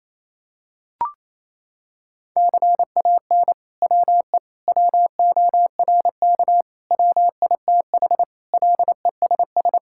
Morse code, heard as a keyed tone, about 28 wpm, 700 Hz. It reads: CAN WE WORK WITH LESS
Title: Sentences from Top 1000 Words - Review Part 5 - 28wpm